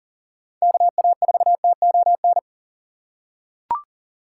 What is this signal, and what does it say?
Morse code, heard as a keyed tone, about 40 wpm, 700 Hz. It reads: KA4TON
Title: USA Call Signs - 6 Characters - 3x Character Spacing - 40wpm